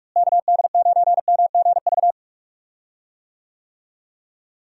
Morse code, heard as a keyed tone, about 46 wpm, 700 Hz. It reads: KD9MGV
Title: USA Call Signs - 6 Characters - Repeated 2x - 45wpm